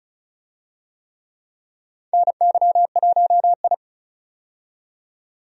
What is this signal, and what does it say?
Morse code, heard as a keyed tone, about 36 wpm, 700 Hz. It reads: NY1I
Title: USA Call Signs - 4 Characters - 35wpm